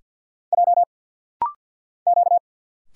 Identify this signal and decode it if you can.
Morse code, heard as a keyed tone, about 50 wpm, 700 Hz. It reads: J Q
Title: Single Letters - Rapid-Fire 50wpm